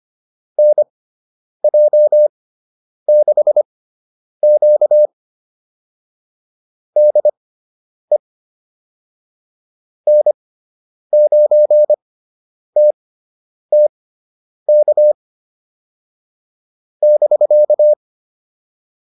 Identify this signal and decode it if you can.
Morse code, heard as a keyed tone, about 25 wpm, spaced as for 9 wpm, 600 Hz. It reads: NJ6Q DE N9TTK <BK>